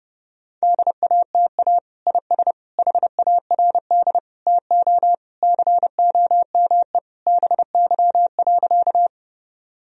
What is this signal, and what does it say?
Morse code, heard as a keyed tone, about 30 wpm, 700 Hz. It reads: DATA IS HARD TO COME BY.
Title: Sentences from Top 1000 Words - Review Part 3 - 30wpm